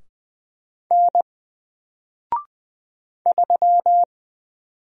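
Morse code, heard as a keyed tone, about 20 wpm, 700 Hz. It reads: N 3
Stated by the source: Single letter and number Morse drill at 20 wpm, from N4LCD.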